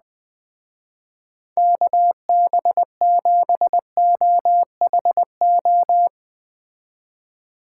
Morse code, heard as a keyed tone, about 21 wpm, 700 Hz. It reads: KB7OHO